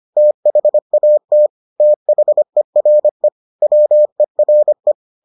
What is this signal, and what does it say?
Morse code, heard as a keyed tone, about 25 wpm, 600 Hz. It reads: THAT THERE WERE